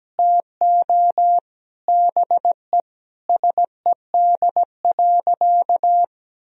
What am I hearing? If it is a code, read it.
Morse code, heard as a keyed tone, about 17 wpm, 700 Hz. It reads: TO BE SED.
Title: Sentences from Top 500 Words and Abbreviations 17wpm